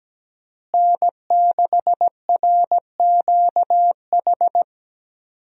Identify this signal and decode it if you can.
Morse code, heard as a keyed tone, about 17 wpm, 700 Hz. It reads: N6RQH
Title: USA Call Signs - 5 Characters - Repeated 3x - 17wpm